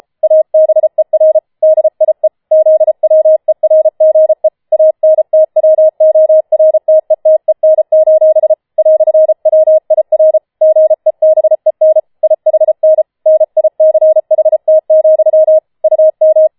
Morse code, heard as a keyed tone, about 33 wpm, 630 Hz. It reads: ABER DIE ZWERGE ANTWORTETEN: "WIR GEBEN IHN NICHT, UM